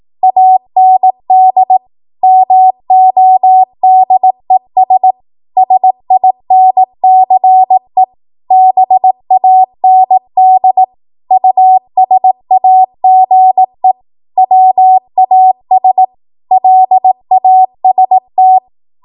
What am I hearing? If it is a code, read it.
Morse code, heard as a keyed tone, about 18 wpm, 750 Hz. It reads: AND MODES SINCE BAND USAGE WAS LAST
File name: Morse / 18WPM